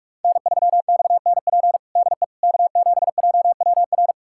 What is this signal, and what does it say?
Morse code, heard as a keyed tone, about 46 wpm, 700 Hz. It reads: N3XNP DE K7JWR
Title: USA Callsign DE Callsign - 5 Characters - 45wpm